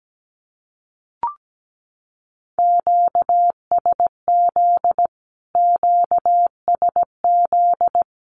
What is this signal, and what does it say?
Morse code, heard as a keyed tone, about 17 wpm, 700 Hz. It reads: QSZ QSZ
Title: Q-Codes Repeated 3x 17wpm